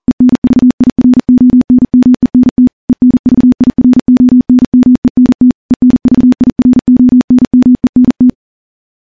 Morse code, heard as a keyed tone, about 40 wpm, 250 Hz. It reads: ENVIRONMENT ENVIRONMENT ENVIRONMENT
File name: morse-training-40WPM